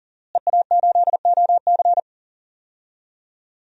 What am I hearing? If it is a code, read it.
Morse code, heard as a keyed tone, about 37 wpm, 700 Hz. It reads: EA8OC
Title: DXCC-100 Call Signs - 5 Characters - Repeated 3x - 40wpm